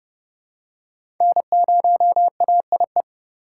Morse code, heard as a keyed tone, about 30 wpm, 700 Hz. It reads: N0AIE
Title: K1USN SST QSO Practice - 30wpm